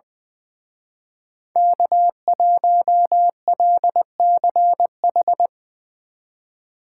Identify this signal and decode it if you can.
Morse code, heard as a keyed tone, about 25 wpm, 700 Hz. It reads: K1LCH
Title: Speed Racing - USA 1x3 Call Signs 20wpm